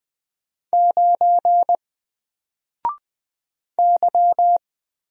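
Morse code, heard as a keyed tone, about 20 wpm, 700 Hz. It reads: 9 Y
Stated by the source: Single letter and number Morse drill at 20 wpm, from N4LCD.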